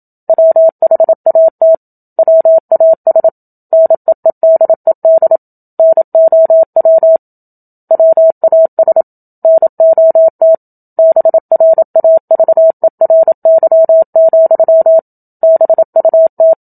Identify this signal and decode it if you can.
Morse code, heard as a keyed tone, about 27 wpm, 650 Hz. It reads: WHAT WAS NEEDED NOW WAS NOT BRAVERY, BUT
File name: Book Two / 01-Chapter 1 - 27 WPM